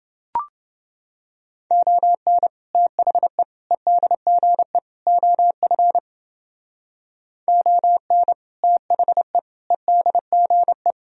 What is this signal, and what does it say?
Morse code, heard as a keyed tone, about 29 wpm, 700 Hz. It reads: ON THE EDGE OF ON THE EDGE